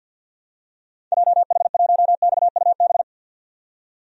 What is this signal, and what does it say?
Morse code, heard as a keyed tone, about 47 wpm, 700 Hz. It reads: JH1XUB
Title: DXCC-100 Call Signs - 6 Characters - 50wpm